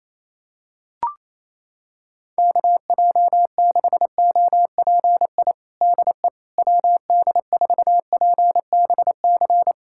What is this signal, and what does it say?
Morse code, heard as a keyed tone, about 27 wpm, 700 Hz. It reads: KJ6OPI DE WD4PBC
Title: USA Callsign DE Callsign - 6 Characters - 28wpm